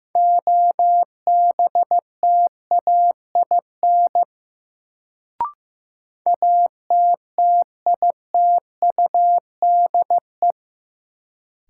Morse code, heard as a keyed tone, about 16 wpm, 700 Hz. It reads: OBTAIN ATTITUDE
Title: Sentences from Top 1300 Words - New Word Drill - 15wpm